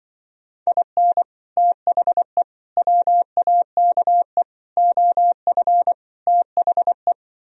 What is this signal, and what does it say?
Morse code, heard as a keyed tone, about 24 wpm, 700 Hz. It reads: IN THE WAKE OF THE
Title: Speed-Racing - Sets of 5 Words 20wpm